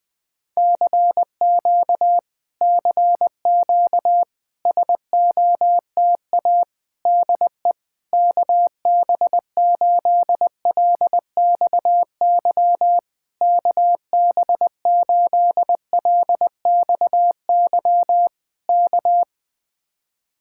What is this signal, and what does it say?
Morse code, heard as a keyed tone, about 20 wpm, 700 Hz. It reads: CQ CQ SOTA DE KB8LXY KB8LXY K
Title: Speed-Racing Common QSO Phrases - 17wpm